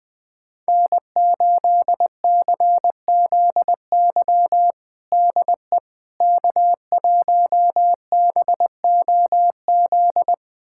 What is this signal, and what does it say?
Morse code, heard as a keyed tone, about 20 wpm, 700 Hz. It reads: N8CZY DE K1BOZ
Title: USA Callsign DE Callsign - 5 Characters - 20wpm